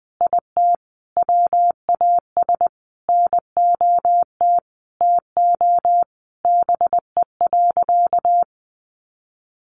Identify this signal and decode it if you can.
Morse code, heard as a keyed tone, about 20 wpm, 700 Hz. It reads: IT WAS NOT TO BE.
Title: Sentences from Top 600 Words 20wpm